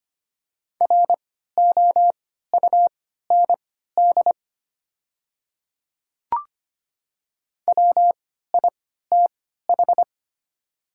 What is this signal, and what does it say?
Morse code, heard as a keyed tone, about 25 wpm, spaced as for 14 wpm, 700 Hz. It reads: ROUND WITH
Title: 3x Character Spacing - Top 200 Words - 25wpm